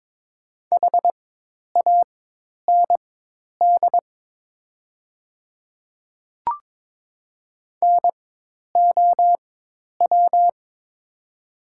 Morse code, heard as a keyed tone, about 22 wpm, spaced as for 10 wpm, 700 Hz. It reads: HAND NOW
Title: 4x Character Spacing - Top 200 Words - 22wpm